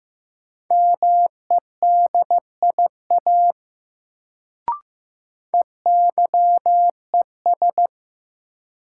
Morse code, heard as a keyed tone, about 16 wpm, 700 Hz. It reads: MEDIA EYES
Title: Sentences from Top 1000 Words - Review Part 4 - New Word Drill - 15wpm